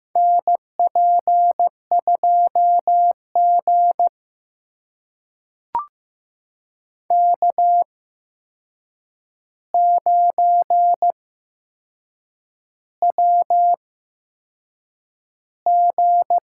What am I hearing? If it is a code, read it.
Morse code, heard as a keyed tone, about 15 wpm, spaced as for 6 wpm, 700 Hz. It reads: NP2G K9WG
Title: USA Call Signs - 4 Characters - 8x Character Spacing - 15wpm